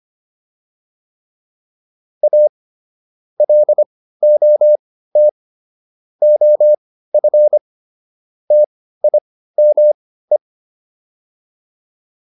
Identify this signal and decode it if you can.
Morse code, heard as a keyed tone, about 25 wpm, spaced as for 15 wpm, 600 Hz. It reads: A LOT OF TIME